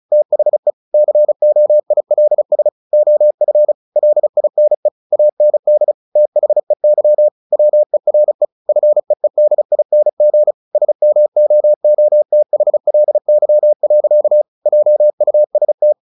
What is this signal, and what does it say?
Morse code, heard as a keyed tone, about 34 wpm, 600 Hz. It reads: THE COILS OF LINE AND THEY WERE FEEDING SMOOTHLY. JUST
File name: the old man and the sea 2 35wpm